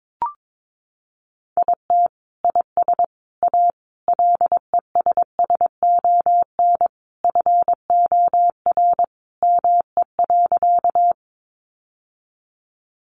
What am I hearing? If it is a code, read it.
Morse code, heard as a keyed tone, about 22 wpm, 700 Hz. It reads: IT IS A LESSON FOR ME.
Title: Sentences from Top 1200 Words 22wpm